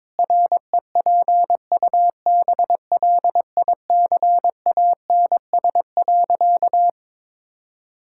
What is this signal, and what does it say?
Morse code, heard as a keyed tone, about 22 wpm, 700 Hz. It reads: REPUBLICANS.
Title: Pangram Challenge v2 - 22wpm